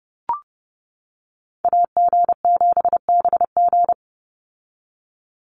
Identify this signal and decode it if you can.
Morse code, heard as a keyed tone, about 30 wpm, 700 Hz. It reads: AG7BG
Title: USA Call Signs 30wpm